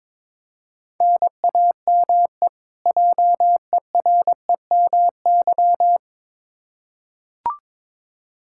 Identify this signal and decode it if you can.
Morse code, heard as a keyed tone, about 22 wpm, 700 Hz. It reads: NAME JEREMY
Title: K1USN SST QSO Practice 22wpm